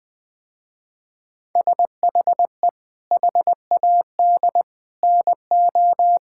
Morse code, heard as a keyed tone, about 20 wpm, 700 Hz. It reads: SHE HAD NO
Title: Sentences with 7-Word Phrases - Spoken Beginning - 20wpm